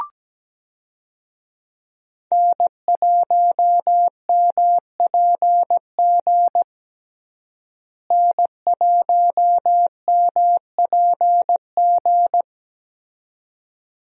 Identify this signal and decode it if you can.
Morse code, heard as a keyed tone, about 17 wpm, 700 Hz. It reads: N1MPG N1MPG
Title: USA Call Signs - 5 Characters - Repeated 2x - 17wpm